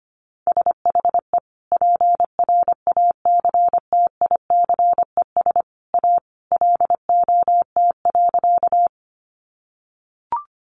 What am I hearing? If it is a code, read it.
Morse code, heard as a keyed tone, about 25 wpm, 700 Hz. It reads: SHE PRACTICES A LOT.
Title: Additional Easy Phrases 25wpm